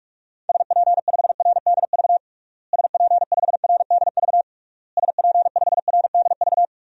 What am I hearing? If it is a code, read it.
Morse code, heard as a keyed tone, about 46 wpm, 700 Hz. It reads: SP5RDV SP5RDV SP5RDV
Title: DXCC-100 Call Signs - 6 Characters - Repeated 3x - 45wpm